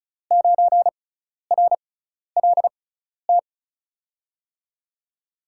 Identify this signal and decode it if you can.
Morse code, heard as a keyed tone, about 34 wpm, spaced as for 12 wpm, 700 Hz. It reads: 9RLT
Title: USA Call Signs - 6 Characters - 6x Character Spacing - 35wpm